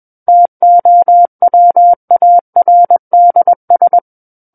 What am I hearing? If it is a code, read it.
Morse code, heard as a keyed tone, about 21 wpm, 700 Hz. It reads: TOWARDS
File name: Book One / 12-Chapter 12 - 21 WPM